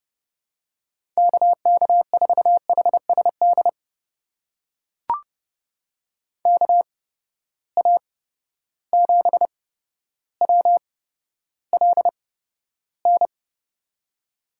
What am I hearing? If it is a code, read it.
Morse code, heard as a keyed tone, about 30 wpm, spaced as for 13 wpm, 700 Hz. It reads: KK4HSD KA7WLN
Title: USA Call Signs - 6 Characters - 8x Character Spacing - 30wpm